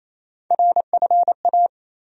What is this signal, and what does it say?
Morse code, heard as a keyed tone, about 28 wpm, 700 Hz. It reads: RFA
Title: USA Call Sign Postfixes - 3 Letters 28wpm